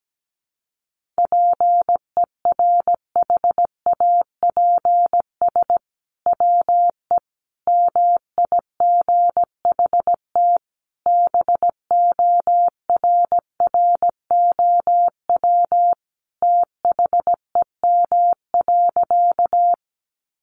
Morse code, heard as a keyed tone, about 17 wpm, 700 Hz. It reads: PERHAPS WE MIGHT BORROW THEM.
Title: Sentences from Top 1500 Words 17wpm